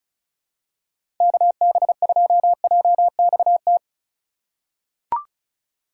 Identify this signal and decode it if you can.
Morse code, heard as a keyed tone, about 36 wpm, 700 Hz. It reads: KB2JXT
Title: USA Call Signs - 6 Characters - 2x Character Spacing - 35wpm